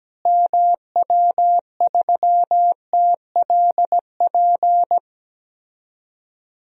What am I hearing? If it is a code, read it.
Morse code, heard as a keyed tone, about 17 wpm, 700 Hz. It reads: MW3TLP
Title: DXCC-100 Call Signs - 6 Characters - Repeated 2x - 17wpm